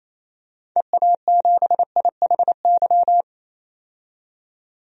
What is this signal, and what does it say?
Morse code, heard as a keyed tone, about 28 wpm, 700 Hz. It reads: EA7IHY
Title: DXCC-100 Call Signs - 6 Characters - 28wpm